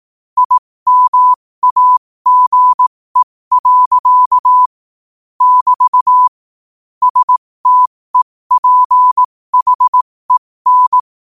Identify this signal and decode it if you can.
Morse code, heard as a keyed tone, about 18 wpm, spaced as for 15 wpm, 1000 Hz. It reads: IMAGE. <BT> STEPHEN
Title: 15 WPM morse code quotes for Fri, 01 Mar 2024